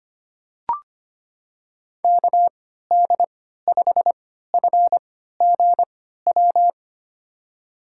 Morse code, heard as a keyed tone, about 25 wpm, spaced as for 14 wpm, 700 Hz. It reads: KD5FGW
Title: USA Call Signs - 6 Characters - 3x Character Spacing - 25wpm